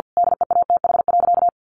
Morse code, heard as a keyed tone, about 50 wpm, 700 Hz. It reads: DEATH.
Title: Sentences from Top 1400 Words 50wpm